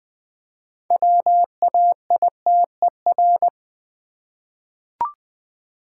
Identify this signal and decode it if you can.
Morse code, heard as a keyed tone, about 20 wpm, 700 Hz. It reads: WAITER
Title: 6 Letter Words - First 3 Letters Given - 20wpm